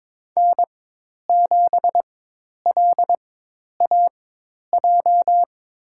Morse code, heard as a keyed tone, about 22 wpm, spaced as for 10 wpm, 700 Hz. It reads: N7LAJ
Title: USA Call Signs - 5 Characters - 4x Character Spacing - 22wpm